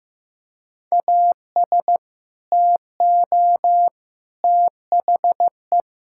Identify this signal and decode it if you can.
Morse code, heard as a keyed tone, about 16 wpm, 700 Hz. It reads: AS TO THE
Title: Sentences from Top 1000 Words - Review Part 5 - 15wpm